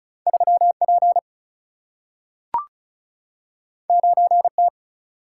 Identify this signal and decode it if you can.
Morse code, heard as a keyed tone, about 36 wpm, 700 Hz. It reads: 3 P 9 T
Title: Single-Letter Single-Number 35wpm